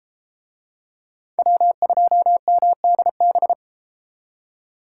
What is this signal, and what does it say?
Morse code, heard as a keyed tone, about 33 wpm, 700 Hz. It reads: W2MDB